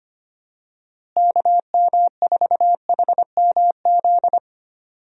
Morse code, heard as a keyed tone, about 25 wpm, 700 Hz. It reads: KM4HMZ
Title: USA Call Signs - 6 Characters - Repeated 3x - 25wpm